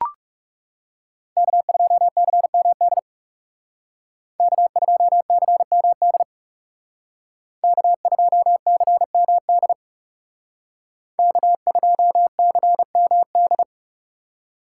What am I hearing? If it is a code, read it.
Morse code, heard as a keyed tone, about 38 wpm, 700 Hz. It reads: K2CMD K2CMD K2CMD K2CMD